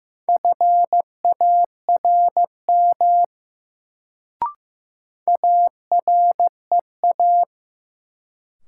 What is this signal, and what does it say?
Morse code, heard as a keyed tone, about 15 wpm, 700 Hz. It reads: FARM AREA
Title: Sentences from Top 1000 Words - Review Part 2 - New Word Drill - 15wpm